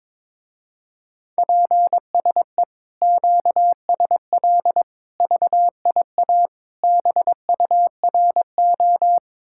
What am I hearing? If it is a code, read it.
Morse code, heard as a keyed tone, about 23 wpm, 700 Hz. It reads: PSE QSL VIA BURO
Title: Speed-Racing Common QSO Phrases - 22wpm